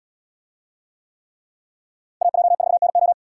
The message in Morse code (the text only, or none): AJ4NO